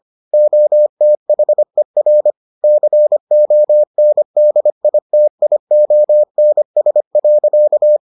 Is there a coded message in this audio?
Yes: Morse code from 0.3 to 8.0 s, heard as a keyed tone, about 25 wpm, 600 Hz. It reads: OTHER CONDITIONS.